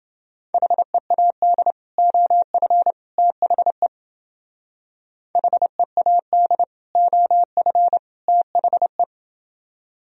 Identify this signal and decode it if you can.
Morse code, heard as a keyed tone, about 29 wpm, 700 Hz. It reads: HEAD OF THE HEAD OF THE